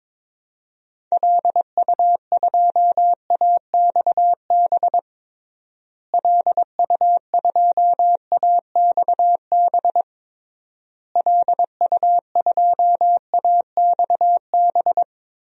Morse code, heard as a keyed tone, about 21 wpm, 700 Hz. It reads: LU2AXB LU2AXB LU2AXB
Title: DXCC-100 Call Signs - 6 Characters - Repeated 3x - 22wpm